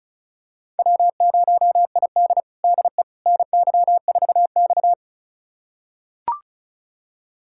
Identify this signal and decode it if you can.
Morse code, heard as a keyed tone, about 36 wpm, 700 Hz. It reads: W0ID DE NY4X
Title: USA Callsign DE Callsign - 4 Characters - 35wpm